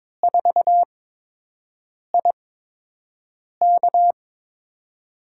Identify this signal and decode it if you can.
Morse code, heard as a keyed tone, about 22 wpm, spaced as for 6 wpm, 700 Hz. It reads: 4IK
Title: USA Call Signs - 8x Character Spacing - 22wpm